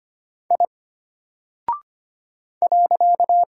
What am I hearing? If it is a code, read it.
Morse code, heard as a keyed tone, about 25 wpm, 700 Hz. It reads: I .
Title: Single Character (Letter:Number:Punctuation:Prosign) Alt 25wpm